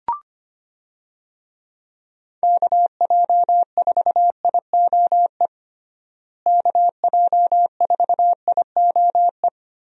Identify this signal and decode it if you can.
Morse code, heard as a keyed tone, about 25 wpm, 700 Hz. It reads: KJ4IOE KJ4IOE
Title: USA Call Signs - 6 Characters - Repeated 2x - 25wpm